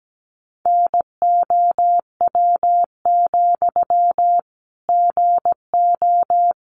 Morse code, heard as a keyed tone, about 17 wpm, 700 Hz. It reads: NOW, GO
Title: Sentences from Top 200 Words 17wpm